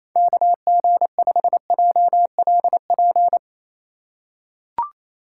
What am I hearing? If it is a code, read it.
Morse code, heard as a keyed tone, about 28 wpm, 690 Hz. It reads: KG5JLP
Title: USA Call Signs - 6 Characters - 2x Character Spacing - 28wpm